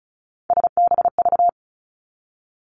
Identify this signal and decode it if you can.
Morse code, heard as a keyed tone, about 36 wpm, 700 Hz. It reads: SBV
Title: Three Challenging Letter-Number Combinations 35wpm